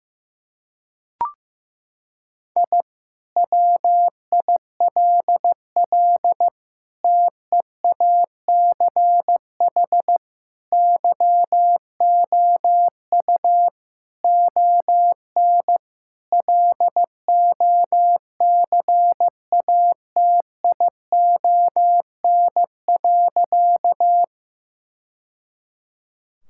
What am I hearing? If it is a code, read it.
Morse code, heard as a keyed tone, about 15 wpm, 700 Hz. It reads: I WILL TEACH YOU ON LOCATION.
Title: Sentences from Top 1000 Words - Review Part 6 - 15wpm